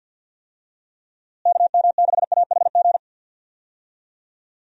Morse code, heard as a keyed tone, about 52 wpm, 700 Hz. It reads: KM6AHG
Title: USA Call Signs - 6 Characters - Repeated 2x - 50wpm